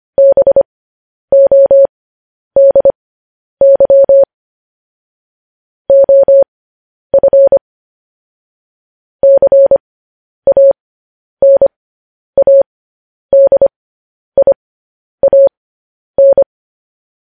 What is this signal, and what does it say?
Morse code, heard as a keyed tone, about 25 wpm, spaced as for 10 wpm, 550 Hz. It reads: BODY OF CANADIAN